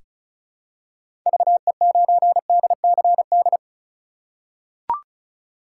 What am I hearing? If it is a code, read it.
Morse code, heard as a keyed tone, about 36 wpm, 700 Hz. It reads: VE9DCD
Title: K1USN SST QSO Practice - 35wpm